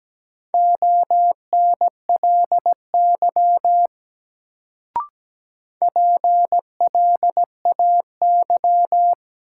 Morse code, heard as a keyed tone, about 17 wpm, 700 Hz. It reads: ONLY PLAY
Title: Sentences from Top 200 Words - New Word Drill - 17wpm